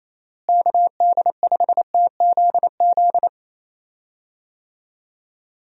Morse code, heard as a keyed tone, about 28 wpm, 700 Hz. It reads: KD5TZZ
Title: USA Call Signs - 6 Characters - Repeated 2x - 28wpm